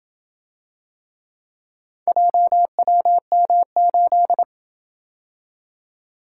Morse code, heard as a keyed tone, about 27 wpm, 700 Hz. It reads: JWM8